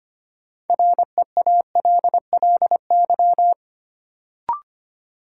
Morse code, heard as a keyed tone, about 25 wpm, 700 Hz. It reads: REALLY
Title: Sentences from Top 600 Words - New Word Drill - 25wpm